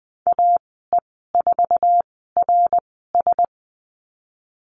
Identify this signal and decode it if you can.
Morse code, heard as a keyed tone, about 20 wpm, spaced as for 14 wpm, 700 Hz. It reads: AE4RS
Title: 062 Lesson 11g - Letter h - Call Signs 20wpm